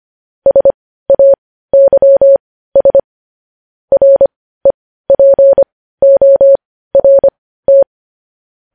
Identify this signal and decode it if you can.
Morse code, heard as a keyed tone, about 25 wpm, spaced as for 15 wpm, 550 Hz. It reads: SAYS REPORT
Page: news headlines in morse code at 15 wpm